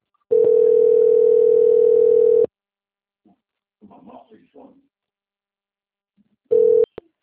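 Ringback tone (North America), 440 + 480 Hz.
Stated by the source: Presidente Prudente - SP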